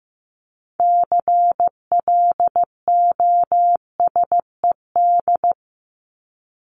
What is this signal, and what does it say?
Morse code, heard as a keyed tone, about 15 wpm, 700 Hz. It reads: CLOSED
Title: ED Suffix - Top 100 Words with Base Words 15wpm